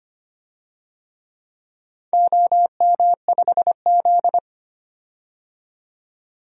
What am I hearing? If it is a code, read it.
Morse code, heard as a keyed tone, about 25 wpm, 700 Hz. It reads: OM5Z
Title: DXCC-100 Call Signs - 4 Characters - 25wpm